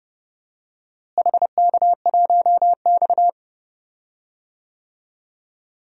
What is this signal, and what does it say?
Morse code, heard as a keyed tone, about 30 wpm, 700 Hz. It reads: HK1X
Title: DXCC-100 Call Signs - 4 Characters - Repeated 2x - 30wpm